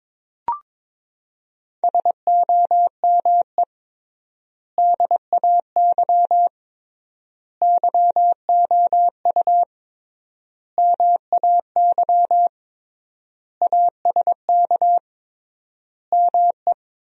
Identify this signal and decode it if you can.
Morse code, heard as a keyed tone, about 22 wpm, 700 Hz. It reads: SOME DAY YOU MAY ASK ME
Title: 3x Word-Spacing - Sentences from Top 200 Words - 22wpm